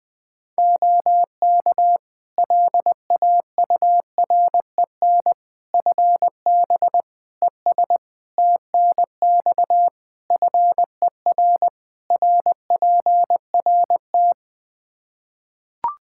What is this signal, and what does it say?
Morse code, heard as a keyed tone, about 20 wpm, 700 Hz. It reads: OK LAUREN FB ES TNX FER RPRT
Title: Practice QSO - Exchange 2 - Info - 20wpm